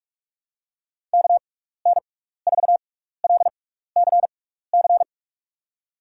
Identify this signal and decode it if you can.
Morse code, heard as a keyed tone, about 46 wpm, spaced as for 16 wpm, 700 Hz. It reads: KN4LCC
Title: USA Call Signs - 6x Character Spacing - 45wpm